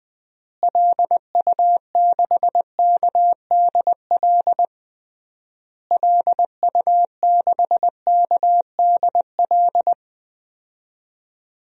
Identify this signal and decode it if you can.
Morse code, heard as a keyed tone, about 20 wpm, 700 Hz. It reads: LU6KDL LU6KDL
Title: DXCC-100 Call Signs - 6 Characters - Repeated 3x - 20wpm